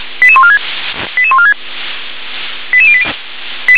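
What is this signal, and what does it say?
Five-tone selective call (ZVEI), heard as quick runs of five tones, tones of about 70 ms. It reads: 88146 88146 88080
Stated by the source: ZVEI five-tone selective call received on SDR.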